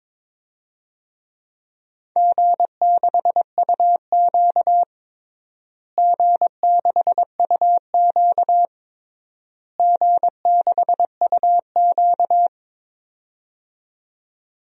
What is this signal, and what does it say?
Morse code, heard as a keyed tone, about 23 wpm, 700 Hz. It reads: G6UQ G6UQ G6UQ
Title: DXCC-100 Call Signs - 4 Characters - Repeated 3x - 22wpm